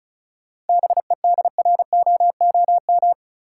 Morse code, heard as a keyed tone, about 36 wpm, 700 Hz. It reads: BEDROOM
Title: Seven Letter Words - 35wpm